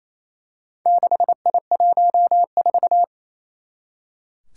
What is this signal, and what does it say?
Morse code, heard as a keyed tone, about 29 wpm, 700 Hz. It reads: 6I14
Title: Four Challenging Letter-Number Combinations 28wpm